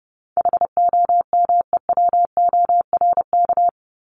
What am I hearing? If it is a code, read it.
Morse code, heard as a keyed tone, about 30 wpm, 700 Hz. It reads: HOMEWORK
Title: Top 1501-1750 Words 30wpm